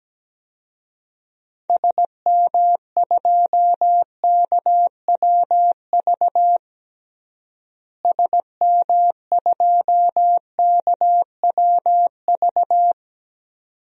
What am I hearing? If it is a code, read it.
Morse code, heard as a keyed tone, about 17 wpm, 700 Hz. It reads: SM2KWV SM2KWV
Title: DXCC-100 Call Signs - 6 Characters - Repeated 2x - 17wpm